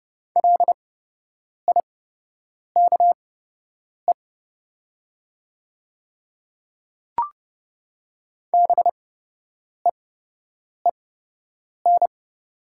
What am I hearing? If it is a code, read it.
Morse code, heard as a keyed tone, about 31 wpm, spaced as for 8 wpm, 700 Hz. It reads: LIKE BEEN
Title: 8x Character Spacing - Top 100 Words - 30wpm